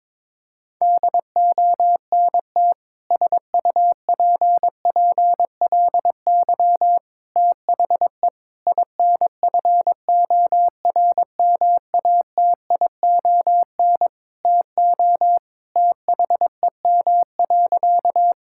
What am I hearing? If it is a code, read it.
Morse code, heard as a keyed tone, about 22 wpm, 700 Hz. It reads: DONT SUPPLY THE INFORMATION TO THEM.
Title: Sentences from Top 1000 Words - Review Part 5 - 22wpm